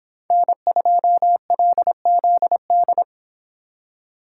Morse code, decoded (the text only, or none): N2LZD